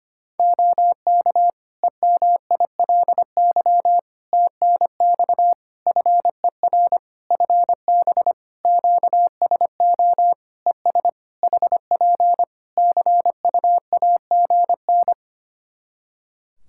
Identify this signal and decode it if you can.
Morse code, heard as a keyed tone, about 25 wpm, 700 Hz. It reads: OK EMILY TNX FER FB QSO ES HP CUAGN
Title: Practice QSO - Exchange 4 - Ending - 25wpm